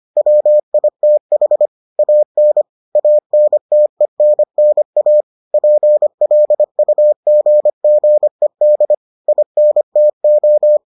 Morse code, heard as a keyed tone, about 25 wpm, 600 Hz. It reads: WITH AN ANTENNA PLUGGED INTO